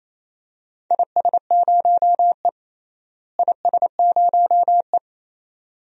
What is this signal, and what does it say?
Morse code, heard as a keyed tone, about 28 wpm, 700 Hz. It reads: IS0E IS0E
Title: DXCC-100 Call Signs - 4 Characters - Repeated 2x - 28wpm